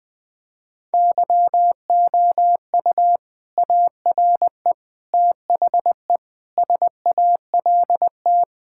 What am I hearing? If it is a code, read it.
Morse code, heard as a keyed tone, about 20 wpm, 700 Hz. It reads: YOU ARE THE SALT
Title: Sentences from Top 1000 Words - Review Part 5 - 20wpm